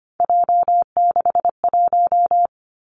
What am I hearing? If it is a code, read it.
Morse code, heard as a keyed tone, about 25 wpm, 700 Hz. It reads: J61
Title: Three Challenging Letter-Number Combinations 25wpm